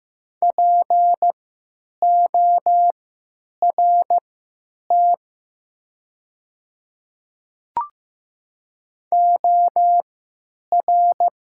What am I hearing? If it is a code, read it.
Morse code, heard as a keyed tone, about 15 wpm, spaced as for 9 wpm, 700 Hz. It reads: PORT OR
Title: 3x Character Spacing - Top 200 Words - 15wpm